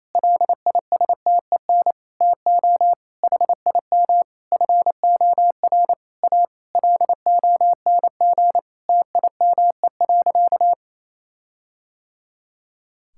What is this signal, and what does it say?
Morse code, heard as a keyed tone, about 27 wpm, 700 Hz. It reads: LISTEN TO HIM FOR A LONG TIME.
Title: Sentences from Top 1600 Words 28wpm